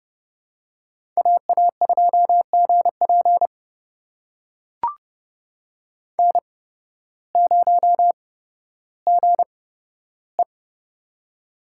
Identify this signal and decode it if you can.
Morse code, heard as a keyed tone, about 32 wpm, spaced as for 14 wpm, 700 Hz. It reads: AA2GP N0GE
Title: USA Call Signs - 8x Character Spacing - 30wpm